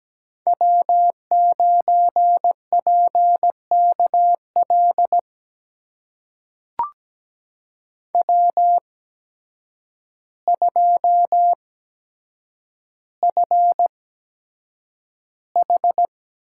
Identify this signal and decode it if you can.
Morse code, heard as a keyed tone, about 17 wpm, spaced as for 8 wpm, 700 Hz. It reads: W9PKL W2FH
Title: USA Call Signs - 5 Characters - 8x Character Spacing - 17wpm